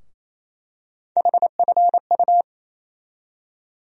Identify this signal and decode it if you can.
Morse code, heard as a keyed tone, about 28 wpm, 700 Hz. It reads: HFU
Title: USA Call Sign Postfixes - 3 Letters 28wpm